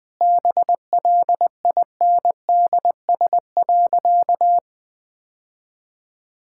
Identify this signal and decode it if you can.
Morse code, heard as a keyed tone, about 20 wpm, 700 Hz. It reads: BLINDS.
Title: Easy Sentences with Hints - 20wpm